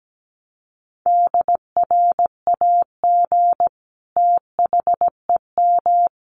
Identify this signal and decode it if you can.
Morse code, heard as a keyed tone, about 17 wpm, 700 Hz. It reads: DRAG THEM
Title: Sentences from Top 1500 Words 17wpm